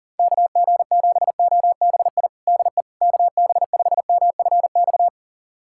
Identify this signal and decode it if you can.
Morse code, heard as a keyed tone, about 40 wpm, 700 Hz. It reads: KG7OBI DE KB5MFX
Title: USA Callsign DE Callsign - 6 Characters - 40wpm